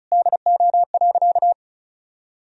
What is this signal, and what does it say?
Morse code, heard as a keyed tone, about 34 wpm, 700 Hz. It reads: DO.
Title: Sentences from Top 100 Words and Abbreviations 35wpm